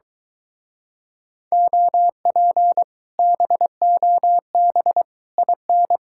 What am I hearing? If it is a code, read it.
Morse code, heard as a keyed tone, about 22 wpm, 700 Hz. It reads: OP BOB IN